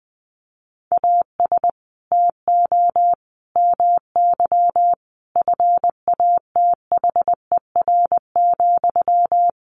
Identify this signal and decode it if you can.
Morse code, heard as a keyed tone, about 20 wpm, 700 Hz. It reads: AS TO MY FATHER,
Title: Sentences from Top 600 Words 20wpm